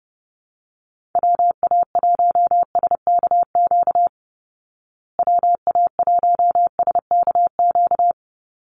Morse code, heard as a keyed tone, about 30 wpm, 700 Hz. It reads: WA1SKQ WA1SKQ
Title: Call Signs Repeated 2x - 30wpm